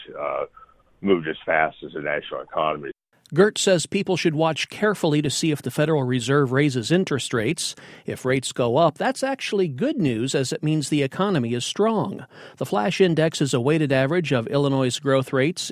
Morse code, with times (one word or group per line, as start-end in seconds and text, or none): none